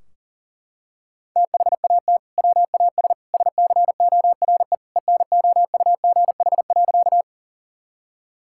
Morse code, heard as a keyed tone, about 37 wpm, 700 Hz. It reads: THAT WAS SCORE ENOUGH.
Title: Sentences from Top 1600 Words 40wpm